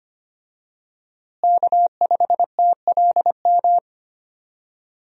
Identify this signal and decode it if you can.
Morse code, heard as a keyed tone, about 25 wpm, 700 Hz. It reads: K5TLM
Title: USA Call Signs - 5 Characters - Repeated 3x - 25wpm